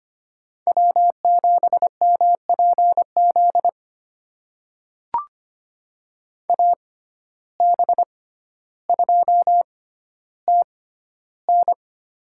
Morse code, heard as a keyed tone, about 25 wpm, spaced as for 13 wpm, 700 Hz. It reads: W7MPZ AB2TN
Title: USA Call Signs - 5 Characters - 6x Character Spacing - 25wpm